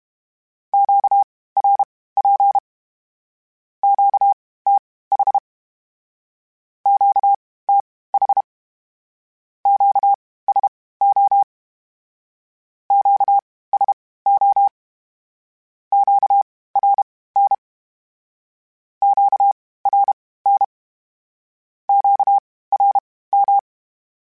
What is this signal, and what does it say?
Morse code, heard as a keyed tone, about 32 wpm, spaced as for 18 wpm, 800 Hz. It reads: QRP QTH QTH QSO QSO QRN QRN QRM
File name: (spaced) / Q-Calls Spaced-mp3s / Q Calls Twice sp @30